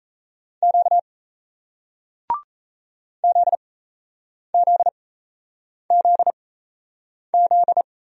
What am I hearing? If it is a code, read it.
Morse code, heard as a keyed tone, about 37 wpm, spaced as for 8 wpm, 700 Hz. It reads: Q ZZZZ